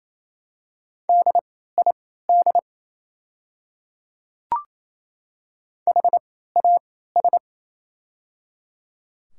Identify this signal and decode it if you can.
Morse code, heard as a keyed tone, about 28 wpm, spaced as for 16 wpm, 700 Hz. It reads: DID HAS
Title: 3x Character Spacing - Top 100 Words - 28wpm